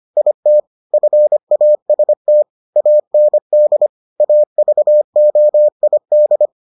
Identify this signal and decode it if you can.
Morse code, heard as a keyed tone, about 25 wpm, 600 Hz. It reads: IT FAST AND AVOID